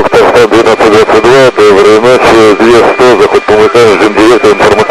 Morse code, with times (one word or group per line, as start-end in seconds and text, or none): none